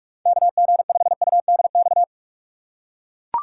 Morse code, heard as a keyed tone, about 42 wpm, 700 Hz. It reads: KG5UDX
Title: USA Call Signs - 3x Character Spacing - 45wpm